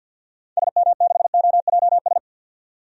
Morse code, heard as a keyed tone, about 53 wpm, 700 Hz. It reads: SM6OJS
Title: DXCC-100 Call Signs - 6 Characters - 50wpm